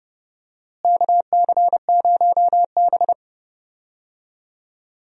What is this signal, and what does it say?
Morse code, heard as a keyed tone, about 30 wpm, 700 Hz. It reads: KC0B
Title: USA Call Signs - 4 Characters - Repeated 2x - 30wpm